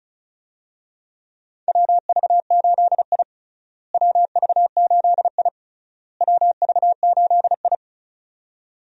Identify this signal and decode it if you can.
Morse code, heard as a keyed tone, about 34 wpm, 700 Hz. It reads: WV8I WV8I WV8I
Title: USA Call Signs - 4 Characters - Repeated 3x - 35wpm